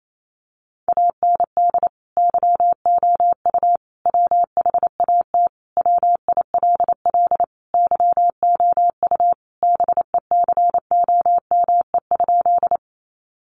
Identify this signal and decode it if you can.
Morse code, heard as a keyed tone, about 28 wpm, 700 Hz. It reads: AND YOU WHAT WILL YOU BECOME?
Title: Sentences from Top 600 Words 28wpm